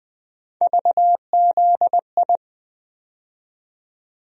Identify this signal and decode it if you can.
Morse code, heard as a keyed tone, about 20 wpm, 700 Hz. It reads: VZI
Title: USA Call Sign Postfixes - 3 Letters 20wpm